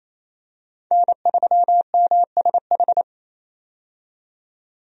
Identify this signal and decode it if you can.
Morse code, heard as a keyed tone, about 28 wpm, 700 Hz. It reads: N3MSH
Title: USA Call Signs - 5 Characters - 28wpm